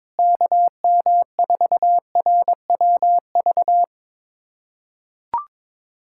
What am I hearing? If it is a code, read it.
Morse code, heard as a keyed tone, about 22 wpm, 700 Hz. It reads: KM4RWV
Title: USA Callsign DE Callsign 22wpm